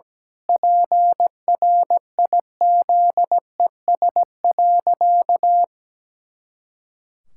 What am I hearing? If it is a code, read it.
Morse code, heard as a keyed tone, about 17 wpm, 700 Hz. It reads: PRIZES.
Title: Pangram Challenge v2 - 17wpm